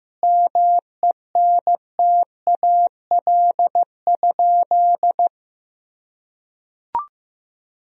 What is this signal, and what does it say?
Morse code, heard as a keyed tone, about 15 wpm, 700 Hz. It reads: MENTAL?
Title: Sentences from Top 1000 Words - Review Part 4 - 15wpm